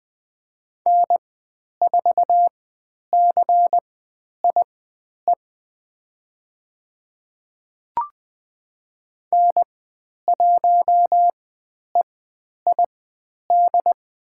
Morse code, heard as a keyed tone, about 20 wpm, spaced as for 10 wpm, 700 Hz. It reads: N4CIE N1EID
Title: 109 Lesson 18g - Letter m - Call Signs 20wpm